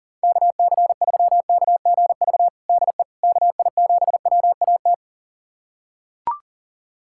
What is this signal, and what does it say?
Morse code, heard as a keyed tone, about 40 wpm, 700 Hz. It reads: KC3KGV DE KI7WAT
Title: USA Callsign DE Callsign 40wpm